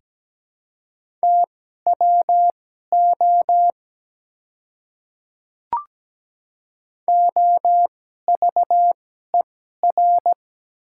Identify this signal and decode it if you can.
Morse code, heard as a keyed tone, about 17 wpm, spaced as for 12 wpm, 700 Hz. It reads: TWO OVER
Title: 2x Character Spacing - Top 200 Words - 17wpm